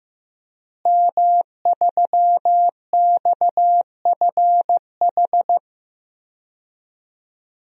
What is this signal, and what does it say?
Morse code, heard as a keyed tone, about 15 wpm, 700 Hz. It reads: M3XFH
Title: DXCC-100 Call Signs - 5 Characters - Repeated 2x - 15wpm